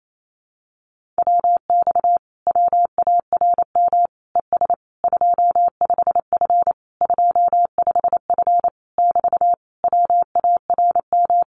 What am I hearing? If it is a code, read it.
Morse code, heard as a keyed tone, about 28 wpm, 700 Hz. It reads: WX WARM ES 25F 25F <BT> WARM
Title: Common QSO Phrases Encore 28wpm